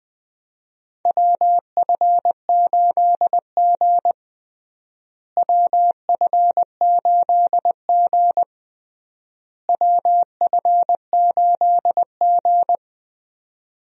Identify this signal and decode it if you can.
Morse code, heard as a keyed tone, about 20 wpm, 700 Hz. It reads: WF8G WF8G WF8G
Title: USA Call Signs - 4 Characters - Repeated 3x - 20wpm